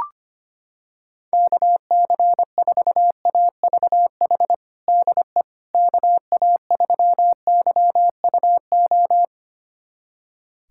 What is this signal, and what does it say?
Morse code, heard as a keyed tone, about 25 wpm, 700 Hz. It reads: KC4AVH DE KA3YUO
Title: USA Callsign DE Callsign - 6 Characters - 25wpm